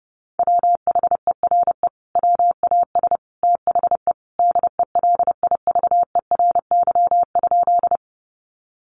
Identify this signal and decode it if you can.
Morse code, heard as a keyed tone, about 30 wpm, 700 Hz. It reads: WHERE WAS THE DELIVERY?
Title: Sentences from Top 1500 Words 30wpm